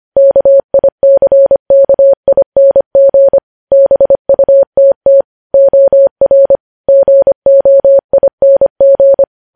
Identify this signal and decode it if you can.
Morse code, heard as a keyed tone, about 25 wpm, 550 Hz. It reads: KICKING BUTT OR GOING
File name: podcast / News headlines-1x-25x25